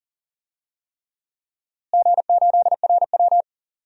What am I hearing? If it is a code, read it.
Morse code, heard as a keyed tone, about 40 wpm, 700 Hz. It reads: G8RW
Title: DXCC-100 Call Signs - 4 Characters - Repeated 3x - 40wpm